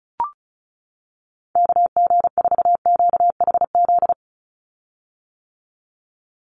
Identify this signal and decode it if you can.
Morse code, heard as a keyed tone, about 36 wpm, 700 Hz. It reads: KG4QHZ
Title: USA Call Signs 35wpm